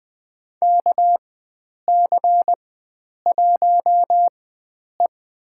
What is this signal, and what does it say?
Morse code, heard as a keyed tone, about 20 wpm, spaced as for 9 wpm, 700 Hz. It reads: KC1E
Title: USA Call Signs - 4x Character Spacing - 20wpm